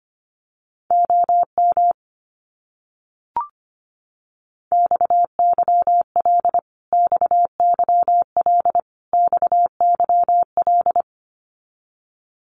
Morse code, heard as a keyed tone, about 25 wpm, 700 Hz. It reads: OM XYL XYL XYL
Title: Abbreviations Repeated 3x - 25wpm